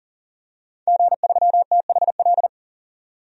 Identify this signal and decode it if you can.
Morse code, heard as a keyed tone, about 40 wpm, 700 Hz. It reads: G3THL
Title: DXCC-100 Call Signs - 5 Characters - 40wpm